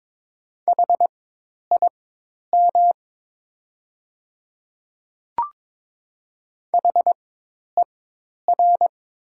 Morse code, heard as a keyed tone, about 22 wpm, spaced as for 10 wpm, 700 Hz. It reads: HIM HER
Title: 4x Character Spacing - Top 100 Words - 22wpm